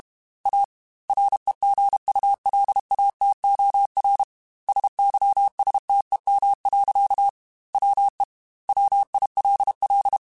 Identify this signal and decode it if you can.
Morse code, heard as a keyed tone, about 32 wpm, 800 Hz. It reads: A REGULATOR SYSTEM. WE WILL
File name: Technical Subjects / Technical Subjects-mp3s / Sw Pwr Sup @30